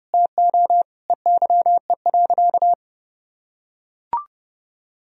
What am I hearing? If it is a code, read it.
Morse code, heard as a keyed tone, about 30 wpm, 700 Hz. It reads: TO EYE.
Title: Sentences from Top 1000 Words - Review Part 2 - 30wpm